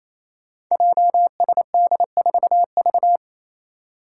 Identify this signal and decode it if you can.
Morse code, heard as a keyed tone, about 29 wpm, 700 Hz. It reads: JSD4V
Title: Five Challenging Letter-Number Combinations 28wpm